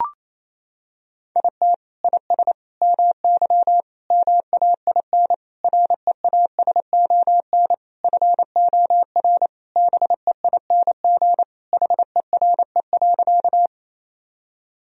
Morse code, heard as a keyed tone, about 28 wpm, 700 Hz. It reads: IT IS MY MAIN REASON FOR BEING HERE.
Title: Sentences from Top 1000 Words - Review Part 2 - 28wpm